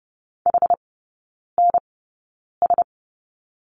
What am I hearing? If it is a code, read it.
Morse code, heard as a keyed tone, about 30 wpm, spaced as for 9 wpm, 700 Hz. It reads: HNS
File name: course / 30:10wpm / 087 Lesson 15e - Number 5 - Three Characters 30wpm